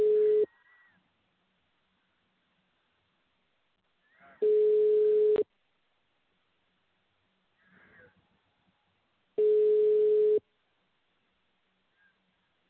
Ringback tone (Europe, 425 Hz).